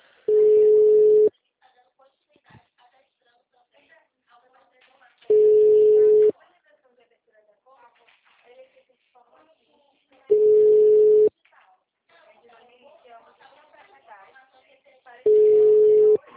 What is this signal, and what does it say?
Ringback tone (Europe, 425 Hz), with speech.
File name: out- -2013- - - .52333